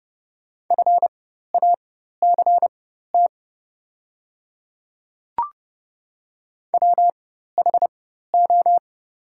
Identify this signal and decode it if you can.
Morse code, heard as a keyed tone, about 30 wpm, spaced as for 14 wpm, 700 Hz. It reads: FACT WHO
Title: 4x Character Spacing - Top 500 Words - 30wpm